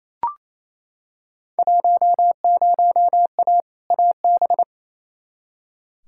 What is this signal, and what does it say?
Morse code, heard as a keyed tone, about 28 wpm, 700 Hz. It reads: 10A AB
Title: Field Day Exchanges Encore - 28wpm